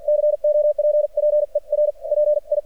Morse code, heard as a keyed tone, about 53 wpm, spaced as for 42 wpm, 600 Hz. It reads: COPPER PI
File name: archived / CRSnet